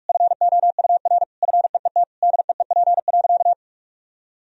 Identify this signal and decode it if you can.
Morse code, heard as a keyed tone, about 46 wpm, 700 Hz. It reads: FOUR FEET DEEP.
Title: Sentences from Top 1000 Words - Review Part 3 - 45wpm